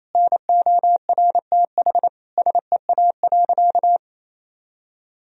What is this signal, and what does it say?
Morse code, heard as a keyed tone, about 27 wpm, 700 Hz. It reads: NORTH SEA.
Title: Sentences from Top 1000 Words - Review Part 1 - 28wpm